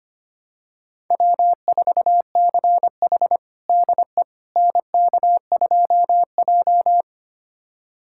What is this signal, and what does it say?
Morse code, heard as a keyed tone, about 25 wpm, 700 Hz. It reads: W4CH DE NK2J
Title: USA Callsign DE Callsign - 4 Characters - 25wpm